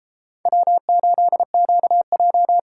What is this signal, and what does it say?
Morse code, heard as a keyed tone, about 28 wpm, 700 Hz. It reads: W8QJ